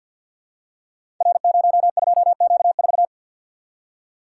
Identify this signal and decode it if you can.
Morse code, heard as a keyed tone, about 46 wpm, 700 Hz. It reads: R02Q4
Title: Code Groups - 5 Letter:Numbers - 50wpm